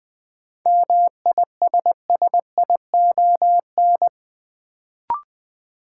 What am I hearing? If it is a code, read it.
Morse code, heard as a keyed tone, about 20 wpm, 700 Hz. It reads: MISSION
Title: Sentences from Top 1600 Words - New Word Drill - 20wpm